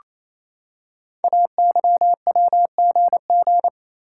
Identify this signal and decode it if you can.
Morse code, heard as a keyed tone, about 27 wpm, 700 Hz. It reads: AYWGG
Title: Code Groups - 5 Letters - 28wpm